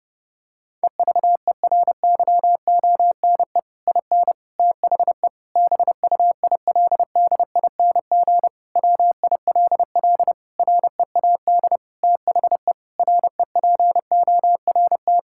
Morse code, heard as a keyed tone, about 30 wpm, 700 Hz. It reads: EVERYONE IN THE BUILDING WILL READ THE REPORT
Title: Sentences from Top 1000 Words - Review Part 6 - 30wpm